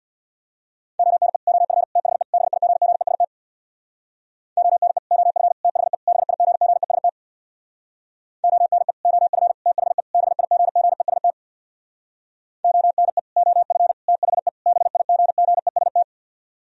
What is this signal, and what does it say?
Morse code, heard as a keyed tone, about 64 wpm, 700 Hz. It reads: ONE OF THE BIGGEST ONE OF THE BIGGEST ONE OF THE BIGGEST ONE OF THE BIGGEST